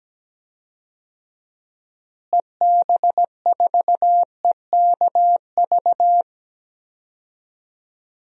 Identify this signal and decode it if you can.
Morse code, heard as a keyed tone, about 17 wpm, 700 Hz. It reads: EB4EKV
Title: DXCC-100 Call Signs - 6 Characters - Repeated 3x - 17wpm